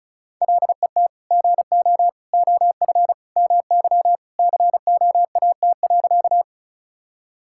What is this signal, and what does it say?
Morse code, heard as a keyed tone, about 36 wpm, 700 Hz. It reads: LET GO OF MY COAT.
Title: Sentences from Top 1600 Words 35wpm